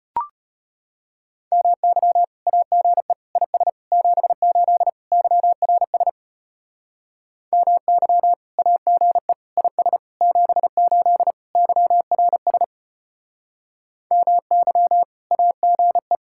Morse code, heard as a keyed tone, about 32 wpm, 700 Hz. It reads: MY AGE IS 78 YRS MY AGE IS 78 YRS MY AGE